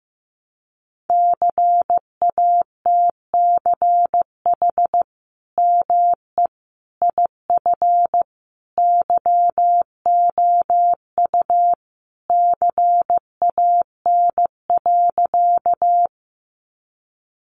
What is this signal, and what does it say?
Morse code, heard as a keyed tone, about 15 wpm, 700 Hz. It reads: CATCH ME IF YOU CAN.
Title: Sentences from Top 1500 Words 15wpm